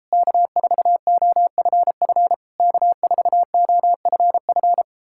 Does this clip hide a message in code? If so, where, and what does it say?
Yes: Morse code from 0.1 to 4.8 s, heard as a keyed tone, about 38 wpm, 700 Hz. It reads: K4OFF K4OFF